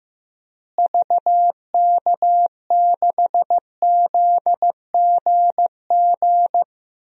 Morse code, heard as a keyed tone, about 15 wpm, 700 Hz. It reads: VK6ZGG
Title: DXCC-100 Call Signs - 6 Characters - Repeated 2x - 15wpm